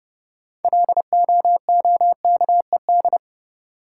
Morse code, heard as a keyed tone, about 30 wpm, 700 Hz. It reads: LOOKED
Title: Sentences from Top 800 Words - New Word Drill - 30wpm